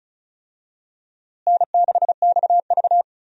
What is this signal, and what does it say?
Morse code, heard as a keyed tone, about 36 wpm, 700 Hz. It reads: N6XV
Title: USA Call Signs - 4 Characters - Repeated 3x - 35wpm